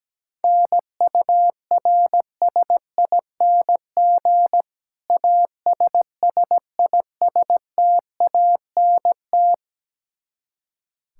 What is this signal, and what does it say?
Morse code, heard as a keyed tone, about 17 wpm, 700 Hz. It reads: NURSING ASSISTANT
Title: Practice QSO - Exchange 3 - More Info - 17wpm